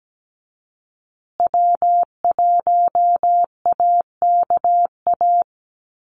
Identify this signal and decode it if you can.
Morse code, heard as a keyed tone, about 17 wpm, 700 Hz. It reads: W1AKA
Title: USA Call Signs Repeated 3x 17wpm